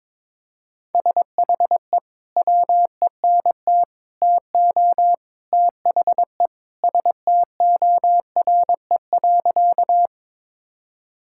Morse code, heard as a keyed tone, about 22 wpm, 700 Hz. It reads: SHE WENT TO THE STORE.
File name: practice / 22wpm / Sentences with 3-Word Phrases - 22wpm